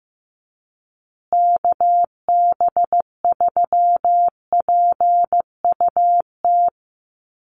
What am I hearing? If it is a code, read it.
Morse code, heard as a keyed tone, about 15 wpm, 700 Hz. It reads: KB3PUT
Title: USA Call Signs Repeated 3x 15wpm